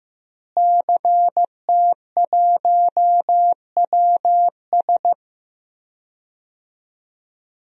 Morse code, heard as a keyed tone, about 16 wpm, 700 Hz. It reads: CT1WS
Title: DXCC-100 Call Signs - 5 Characters - Repeated 2x - 15wpm